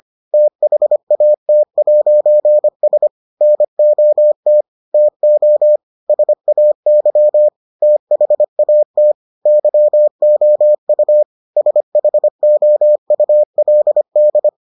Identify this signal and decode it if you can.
Morse code, heard as a keyed tone, about 25 wpm, 600 Hz. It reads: THAT'S NOT TO SAY THAT YOU SHOULD